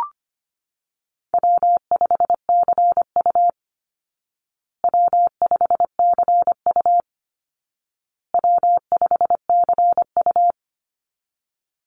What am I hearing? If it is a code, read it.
Morse code, heard as a keyed tone, about 25 wpm, 700 Hz. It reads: W5CU W5CU W5CU
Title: Call Signs Repeated 2x - 25wpm